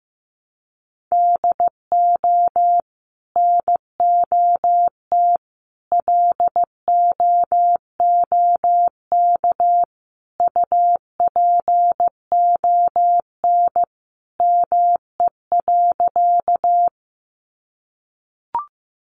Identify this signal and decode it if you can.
Morse code, heard as a keyed tone, about 15 wpm, 700 Hz. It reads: DO NOT LOOK UPON ME.
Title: Sentences from Top 1100 Words 15wpm